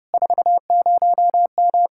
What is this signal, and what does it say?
Morse code, heard as a keyed tone, about 31 wpm, 700 Hz. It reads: 40M